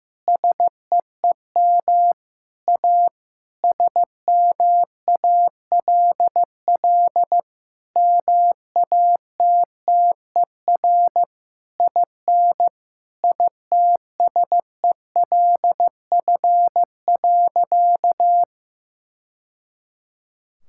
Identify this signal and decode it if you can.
Morse code, heard as a keyed tone, about 15 wpm, 700 Hz. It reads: SEEM A SMALL MATTER IN ITSELF.
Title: Sentences from Top 1000 Words - Review Part 5 - 15wpm